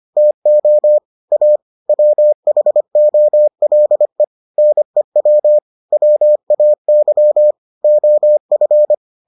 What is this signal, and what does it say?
Morse code, heard as a keyed tone, about 25 wpm, 600 Hz. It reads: TO A WHOLE NEW WAY OF